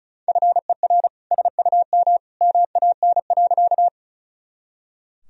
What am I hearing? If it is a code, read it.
Morse code, heard as a keyed tone, about 34 wpm, 690 Hz. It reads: FER SUM MAN.
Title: Sentences from Top 200 Words and Abbreviations 35wpm